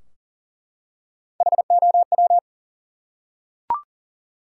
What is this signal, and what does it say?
Morse code, heard as a keyed tone, about 39 wpm, 700 Hz. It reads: HOW
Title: Top 100 Words 40wpm